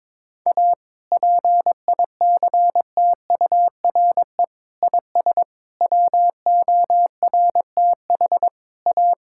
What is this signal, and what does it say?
Morse code, heard as a keyed tone, about 22 wpm, 700 Hz. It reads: A PICTURE IS WORTH A
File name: practice / 22wpm / Sets of 7 Words - 22wpm